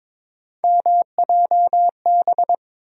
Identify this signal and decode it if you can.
Morse code, heard as a keyed tone, about 22 wpm, 700 Hz. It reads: MJB
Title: USA Call Sign Postfixes - 3 Letters 22wpm